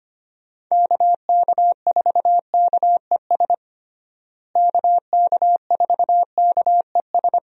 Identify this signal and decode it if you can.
Morse code, heard as a keyed tone, about 25 wpm, 700 Hz. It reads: KK4KES KK4KES
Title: USA Call Signs - 6 Characters - Repeated 2x - 25wpm